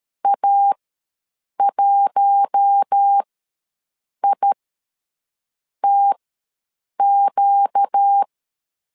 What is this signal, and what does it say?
Morse code, heard as a keyed tone, about 13 wpm, spaced as for 6 wpm, 800 Hz. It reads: A1ITQ
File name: LEZIONI / 4 Q1S ws 12wpm 1827